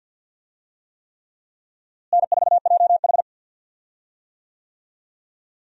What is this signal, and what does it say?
Morse code, heard as a keyed tone, about 52 wpm, 700 Hz. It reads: N4JH
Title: USA Call Signs - 4 Characters - Repeated 3x - 50wpm